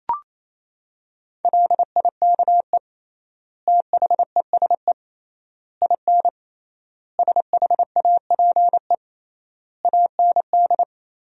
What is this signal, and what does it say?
Morse code, heard as a keyed tone, about 27 wpm, 700 Hz. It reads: LIKE THESE IN SHAPE AND
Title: 3x Word-Spacing - Sentences from Top 500 Words - 28wpm